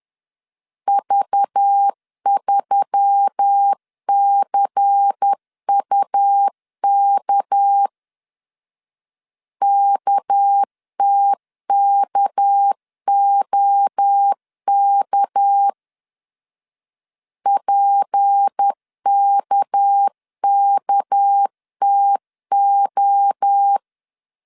Morse code, heard as a keyed tone, about 10 wpm, 800 Hz. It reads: V3CUK KTKOK PKKTO
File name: LEZIONI / 6 PKX0 / 6 PK 10wpm 2436